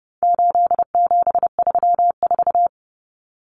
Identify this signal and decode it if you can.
Morse code, heard as a keyed tone, about 29 wpm, 700 Hz. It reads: 8734
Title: Common Cut Numbers - 4-Digit 30wpm